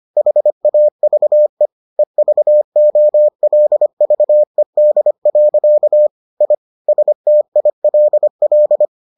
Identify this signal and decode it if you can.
Morse code, heard as a keyed tone, about 25 wpm, 600 Hz. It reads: HAVE EVOLVED. I STILL